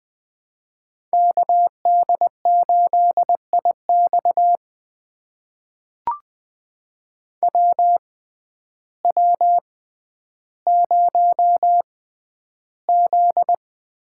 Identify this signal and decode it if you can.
Morse code, heard as a keyed tone, about 20 wpm, spaced as for 11 wpm, 700 Hz. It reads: KD8IX WW0Z
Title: USA Call Signs - 6x Character Spacing - 20wpm